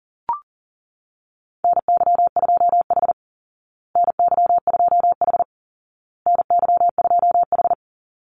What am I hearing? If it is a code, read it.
Morse code, heard as a keyed tone, about 40 wpm, 700 Hz. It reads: NY2H NY2H NY2H
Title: Call Signs Repeated 2x - 40wpm